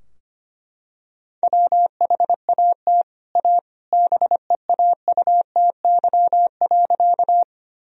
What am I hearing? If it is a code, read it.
Morse code, heard as a keyed tone, about 25 wpm, 700 Hz. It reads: WHAT A BEAUTY.
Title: Sentences from Top 1000 Words - Review Part 2 - 25wpm